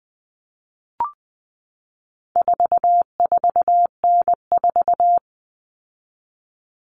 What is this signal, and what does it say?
Morse code, heard as a keyed tone, about 20 wpm, 700 Hz. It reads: 44N4
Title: All Cut Numbers - 4-Digit 20wpm